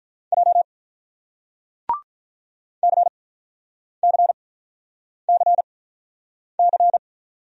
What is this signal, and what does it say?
Morse code, heard as a keyed tone, about 46 wpm, spaced as for 9 wpm, 700 Hz. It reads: J CCCC